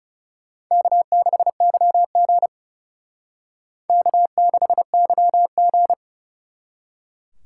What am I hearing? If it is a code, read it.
Morse code, heard as a keyed tone, about 38 wpm, 700 Hz. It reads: K6YG K6YG